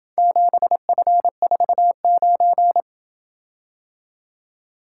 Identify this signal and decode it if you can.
Morse code, heard as a keyed tone, about 27 wpm, 700 Hz. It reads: 7F49